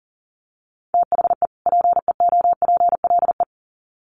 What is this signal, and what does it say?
Morse code, heard as a keyed tone, about 40 wpm, 700 Hz. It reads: THE PEOPLE
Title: Sets of 2 Words 40wpm